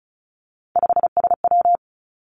Morse code, heard as a keyed tone, about 36 wpm, 700 Hz. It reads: 5SW
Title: Three Challenging Letter-Number Combinations 35wpm